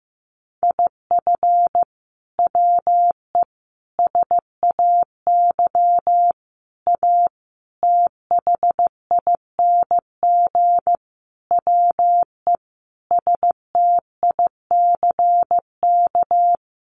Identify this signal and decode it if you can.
Morse code, heard as a keyed tone, about 15 wpm, 700 Hz. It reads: IF WE SAY A THING WE STICK
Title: Sentences from Top 1200 Words 15wpm